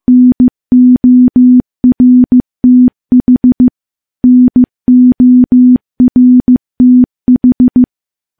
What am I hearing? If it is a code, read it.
Morse code, heard as a keyed tone, about 16 wpm, 250 Hz. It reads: NORTH NORTH
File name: morse-training-15WPM